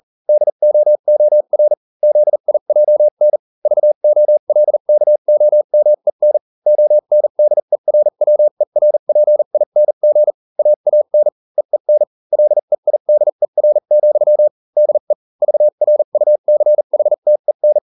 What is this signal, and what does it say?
Morse code, heard as a keyed tone, about 40 wpm, spaced as for 35 wpm, 600 Hz. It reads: DOOR ZIJN VOLKOMEN ONDERWERPING AAN EEN LEIDER, DE VRUCHTEN